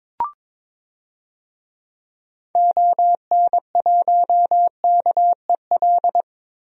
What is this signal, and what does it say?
Morse code, heard as a keyed tone, about 22 wpm, 700 Hz. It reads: ON1KEL
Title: DXCC-100 Call Signs - 6 Characters - Repeated 3x - 22wpm